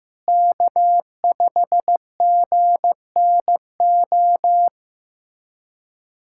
Morse code, heard as a keyed tone, about 15 wpm, 700 Hz. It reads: K5GNO
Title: USA Callsign DE Callsign - 5 Characters - 15wpm